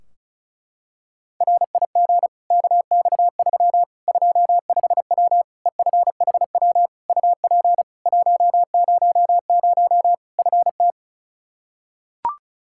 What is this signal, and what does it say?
Morse code, heard as a keyed tone, about 36 wpm, 700 Hz. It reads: RIG KX3 25W EFHW UP 100 FT
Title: Practice QSO - Exchange 2 - Info - 35wpm